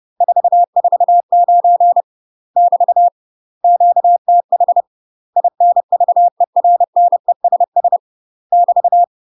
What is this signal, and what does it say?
Morse code, heard as a keyed tone, about 30 wpm, 700 Hz. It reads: 449 <BT> QTH INVERNESS <BT>
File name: UK MASTER QSOs TRAINING / 30WPM / UK MASTER QSOs 0002